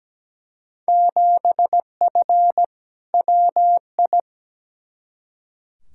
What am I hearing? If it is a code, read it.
Morse code, heard as a keyed tone, about 17 wpm, 700 Hz. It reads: 7F WI
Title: Field Day Exchanges Encore - 17wpm